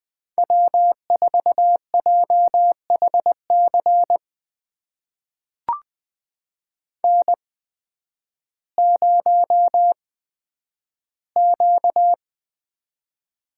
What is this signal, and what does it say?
Morse code, heard as a keyed tone, about 20 wpm, spaced as for 11 wpm, 700 Hz. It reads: W4JHC N0Q
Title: USA Call Signs - 5 Characters - 8x Character Spacing - 20wpm